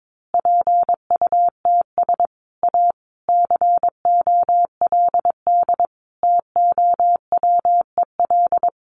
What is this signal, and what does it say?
Morse code, heard as a keyed tone, about 22 wpm, 700 Hz. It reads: PUTS A COLD TOWEL
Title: Easy Phrases 22wpm